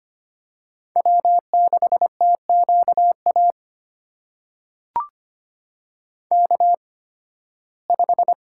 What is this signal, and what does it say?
Morse code, heard as a keyed tone, about 25 wpm, spaced as for 16 wpm, 700 Hz. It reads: W6TQA K5
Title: USA Call Signs - 5 Characters - 8x Character Spacing - 25wpm